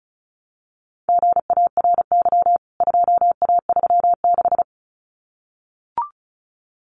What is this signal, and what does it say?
Morse code, heard as a keyed tone, about 34 wpm, 700 Hz. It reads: GARY 2A36
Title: CWT Exchange 35wpm